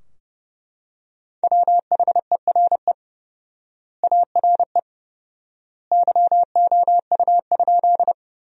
Morse code, heard as a keyed tone, about 30 wpm, 700 Hz. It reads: WHERE ARE YOU?
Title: 4x Word-Spacing - Sentences from Top 200 Words - 30wpm